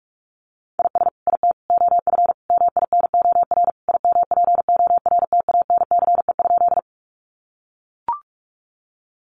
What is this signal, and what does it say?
Morse code, heard as a keyed tone, about 44 wpm, 700 Hz. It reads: IS IT OF MINOR IMPORTANCE?
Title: Sentences from Top 1500 Words 45wpm